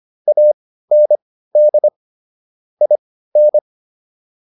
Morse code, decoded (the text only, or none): AND IN